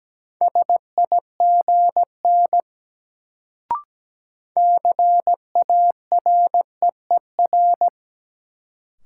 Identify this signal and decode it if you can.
Morse code, heard as a keyed tone, about 17 wpm, 700 Hz. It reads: SIGN CAREER
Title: Sentences from Top 1000 Words - New Word Drill - 17wpm